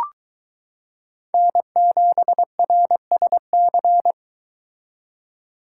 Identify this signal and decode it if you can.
Morse code, heard as a keyed tone, about 23 wpm, 700 Hz. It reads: N7RSC